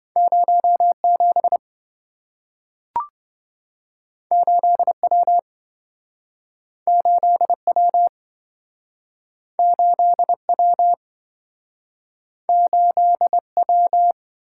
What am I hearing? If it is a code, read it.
Morse code, heard as a keyed tone, about 27 wpm, 700 Hz. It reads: 07 8W 8W 8W 8W